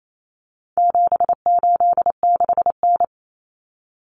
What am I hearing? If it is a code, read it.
Morse code, heard as a keyed tone, about 28 wpm, 700 Hz. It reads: 786N
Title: Common Cut Numbers - 4-Digit 28wpm